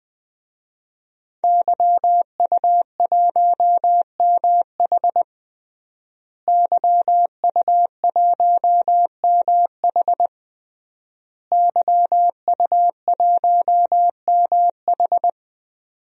Morse code, heard as a keyed tone, about 20 wpm, 700 Hz. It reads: YU1MH YU1MH YU1MH
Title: DXCC-100 Call Signs - 5 Characters - Repeated 3x - 20wpm